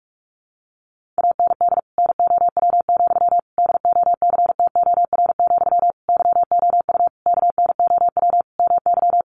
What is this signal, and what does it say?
Morse code, heard as a keyed tone, about 44 wpm, 700 Hz. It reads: AND NOW, DOCTOR, YOU KNOW MY
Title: Sentences from Top 1500 Words 45wpm